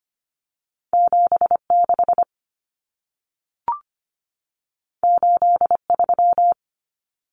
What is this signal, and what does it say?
Morse code, heard as a keyed tone, about 25 wpm, 700 Hz. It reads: 76 83
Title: Numbers - 2-Digit 25wpm